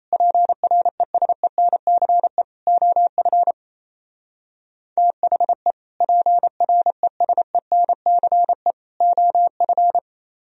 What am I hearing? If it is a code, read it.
Morse code, heard as a keyed tone, about 35 wpm, 700 Hz. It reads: PRESENCE OF THE PRESENCE OF